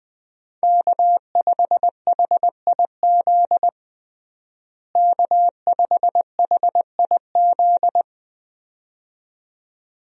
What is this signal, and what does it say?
Morse code, heard as a keyed tone, about 20 wpm, 700 Hz. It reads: K5HIZ K5HIZ
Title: USA Call Signs - 5 Characters - Repeated 2x - 20wpm